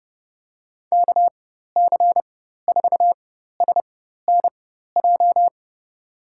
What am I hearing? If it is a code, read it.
Morse code, heard as a keyed tone, about 30 wpm, spaced as for 14 wpm, 700 Hz. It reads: KC4SNJ
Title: USA Call Signs - 6 Characters - 4x Character Spacing - 30wpm